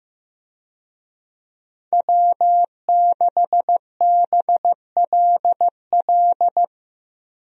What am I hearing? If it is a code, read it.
Morse code, heard as a keyed tone, about 15 wpm, 700 Hz. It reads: W6BLL
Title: USA Call Signs - 5 Characters - Repeated 2x - 15wpm